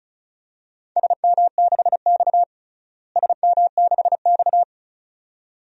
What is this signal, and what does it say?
Morse code, heard as a keyed tone, about 36 wpm, 700 Hz. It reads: SM6X SM6X
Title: DXCC-100 Call Signs - 4 Characters - Repeated 2x - 35wpm